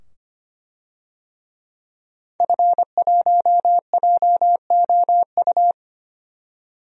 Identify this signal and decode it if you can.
Morse code, heard as a keyed tone, about 25 wpm, 700 Hz. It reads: F1JOU
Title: DXCC-100 Call Signs - 5 Characters - 25wpm